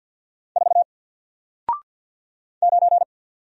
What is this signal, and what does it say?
Morse code, heard as a keyed tone, about 48 wpm, 700 Hz. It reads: 4 9
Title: Single Letter-Number Alt 50wpm